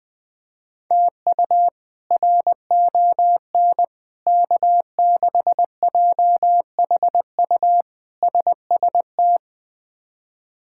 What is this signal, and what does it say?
Morse code, heard as a keyed tone, about 20 wpm, 700 Hz. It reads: TU RON K6JHU SST
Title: K1USN SST QSO Practice - 20wpm